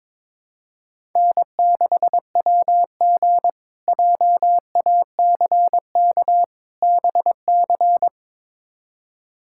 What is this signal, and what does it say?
Morse code, heard as a keyed tone, about 22 wpm, 700 Hz. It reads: N6WG JACK BC
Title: K1USN SST QSO Practice - 22wpm